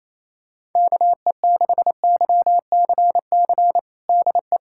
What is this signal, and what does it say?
Morse code, heard as a keyed tone, about 28 wpm, 700 Hz. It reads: KE6YCC DE
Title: USA Callsign DE Callsign 28wpm